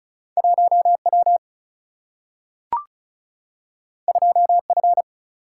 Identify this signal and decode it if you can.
Morse code, heard as a keyed tone, about 36 wpm, 700 Hz. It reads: 1W 2F
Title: Single-Letter Single-Number 35wpm